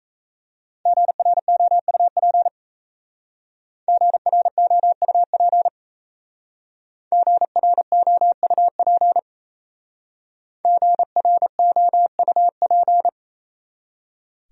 Morse code, heard as a keyed tone, about 35 wpm, 700 Hz. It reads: GROUP GROUP GROUP GROUP
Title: Speed-Racing Top 500 Words 28wpm